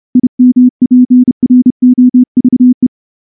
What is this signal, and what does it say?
Morse code, heard as a keyed tone, about 29 wpm, 250 Hz. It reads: IMPROVE
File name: morse-training-30WPM